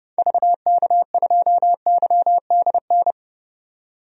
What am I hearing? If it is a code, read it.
Morse code, heard as a keyed tone, about 30 wpm, 700 Hz. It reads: VK2YDN
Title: DXCC-100 Call Signs - 6 Characters - 30wpm